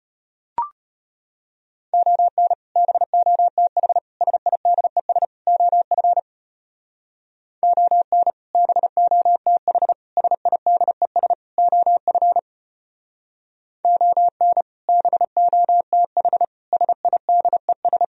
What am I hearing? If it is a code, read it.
Morse code, heard as a keyed tone, about 32 wpm, 700 Hz. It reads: ON BOTH SIDES OF ON BOTH SIDES OF ON BOTH SIDES